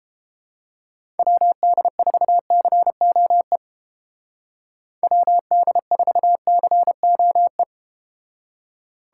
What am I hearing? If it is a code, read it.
Morse code, heard as a keyed tone, about 31 wpm, 700 Hz. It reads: WD4COE WD4COE